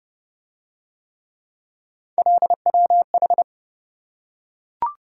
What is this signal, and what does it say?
Morse code, heard as a keyed tone, about 27 wpm, 700 Hz. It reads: LWH